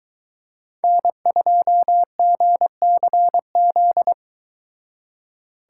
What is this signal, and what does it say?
Morse code, heard as a keyed tone, about 22 wpm, 700 Hz. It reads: N2GCZ